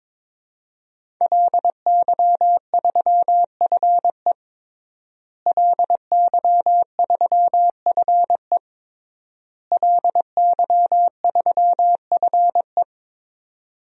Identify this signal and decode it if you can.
Morse code, heard as a keyed tone, about 22 wpm, 700 Hz. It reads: LY3FE LY3FE LY3FE
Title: DXCC-100 Call Signs - 5 Characters - Repeated 3x - 22wpm